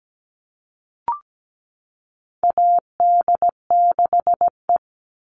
Morse code, heard as a keyed tone, about 17 wpm, 700 Hz. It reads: AD6E
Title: Call Signs Repeated 2x - 17wpm